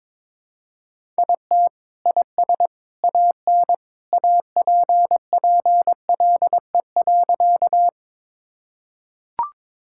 Morse code, heard as a keyed tone, about 22 wpm, 700 Hz. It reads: IT IS AN APPLE.
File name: practice / 22wpm / Sentences with 2-Word Phrases - 22wpm